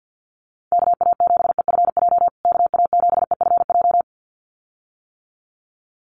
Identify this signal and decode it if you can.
Morse code, heard as a keyed tone, about 50 wpm, 700 Hz. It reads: KA7EFJ KA7EFJ
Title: USA Call Signs Repeated 2x 50wpm